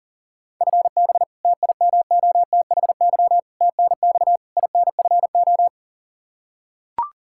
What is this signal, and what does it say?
Morse code, heard as a keyed tone, about 40 wpm, 700 Hz. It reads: FB TIMOTHY TNX INFO
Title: K1USN SST QSO Practice 40wpm